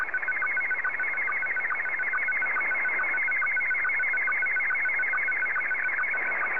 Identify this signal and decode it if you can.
SSTV (slow-scan television), Scottie 1.